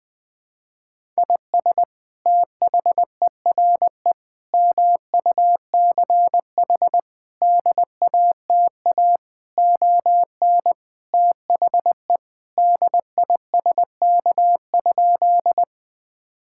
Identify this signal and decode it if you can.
Morse code, heard as a keyed tone, about 21 wpm, 700 Hz. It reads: IS THERE MUCH DATA ON THE DISK?
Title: Sentences from Top 1000 Words - Review Part 6 - 20wpm